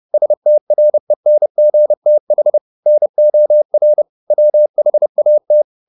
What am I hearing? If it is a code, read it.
Morse code, heard as a keyed tone, about 30 wpm, 600 Hz. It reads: STRENGTH NOR WHAT
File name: the old man and the sea / the old man and the sea 1 30wpm